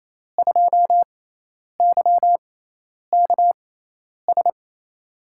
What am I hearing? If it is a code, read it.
Morse code, heard as a keyed tone, about 28 wpm, spaced as for 10 wpm, 700 Hz. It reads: 2YKS
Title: USA Call Signs - 5 Characters - 6x Character Spacing - 28wpm